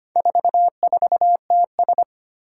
Morse code, heard as a keyed tone, about 25 wpm, 700 Hz. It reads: 44TS
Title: Four Challenging Letter-Number Combinations 25wpm